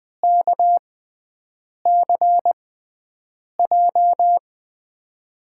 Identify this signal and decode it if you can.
Morse code, heard as a keyed tone, about 20 wpm, spaced as for 7 wpm, 700 Hz. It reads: KCJ
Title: USA Call Signs - 6 Characters - 6x Character Spacing - 20wpm